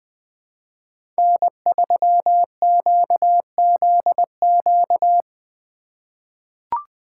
Morse code, heard as a keyed tone, about 20 wpm, 700 Hz. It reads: N3QZQ
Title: USA Call Signs - 5 Characters - 3x Character Spacing - 20wpm